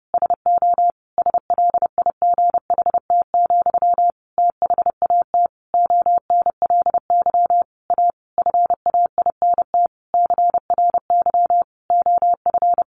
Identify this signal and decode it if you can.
Morse code, heard as a keyed tone, about 30 wpm, 700 Hz. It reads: SO SLIGHT, THAT ONLY A FAINT CRY OF
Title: A Sound Of Thunder 30wpm